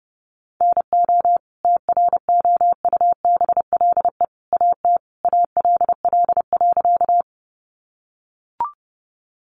Morse code, heard as a keyed tone, about 30 wpm, 700 Hz. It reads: NO TROUBLE AT ALL.
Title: Sentences from Top 1200 Words 30wpm